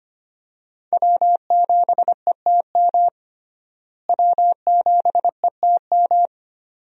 Morse code, heard as a keyed tone, about 25 wpm, 700 Hz. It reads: W7ETM W7ETM
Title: USA Call Signs - 5 Characters - Repeated 2x - 25wpm